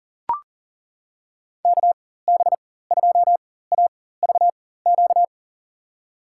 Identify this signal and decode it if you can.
Morse code, heard as a keyed tone, about 37 wpm, spaced as for 18 wpm, 700 Hz. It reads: KB2AVQ
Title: USA Call Signs - 6 Characters - 4x Character Spacing - 40wpm